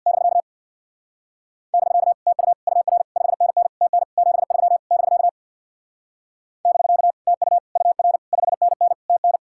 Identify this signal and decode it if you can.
Morse code, heard as a keyed tone, about 58 wpm, 700 Hz. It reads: <BK> <BK> TU UR 5NN TN 73 <BK> <BK> TU UR 5NN TN
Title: Speed-Racing Common QSO Phrases - 50wpm